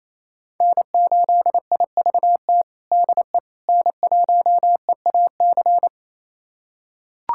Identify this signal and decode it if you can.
Morse code, heard as a keyed tone, about 27 wpm, 700 Hz. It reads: N8IVT DE N1EAC
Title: USA Callsign DE Callsign - 5 Characters - 28wpm